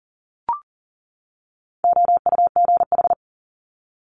Morse code, heard as a keyed tone, about 38 wpm, 700 Hz. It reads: OUGH
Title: Multi-letter Phonemes 40wpm